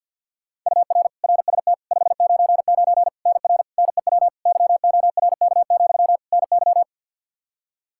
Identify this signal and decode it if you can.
Morse code, heard as a keyed tone, about 51 wpm, 700 Hz. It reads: UR RST 599 NR NEW YORK, NY
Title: K1USN SST QSO Practice 50wpm